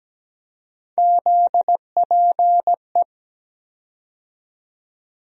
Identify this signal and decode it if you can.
Morse code, heard as a keyed tone, about 17 wpm, 700 Hz. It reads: ZPE
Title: USA Call Sign Postfixes - 3 Letters 17wpm